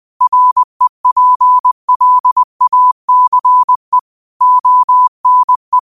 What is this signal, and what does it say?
Morse code, heard as a keyed tone, about 20 wpm, 1000 Hz. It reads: REPLACE ONE